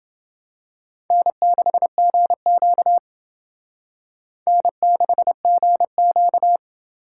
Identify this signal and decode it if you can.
Morse code, heard as a keyed tone, about 28 wpm, 700 Hz. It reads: N6GQ N6GQ